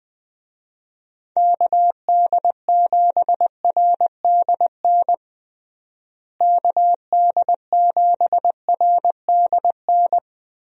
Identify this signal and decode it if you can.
Morse code, heard as a keyed tone, about 20 wpm, 700 Hz. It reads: KD7RDN KD7RDN
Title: USA Call Signs - 6 Characters - Repeated 3x - 20wpm